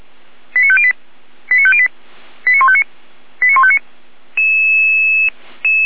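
Five-tone selective call (ZVEI), heard as quick runs of five tones, tones of about 70 ms. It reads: 89508 89508 89159 89159
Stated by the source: ZVEI five-tone call 89160 received by SDR.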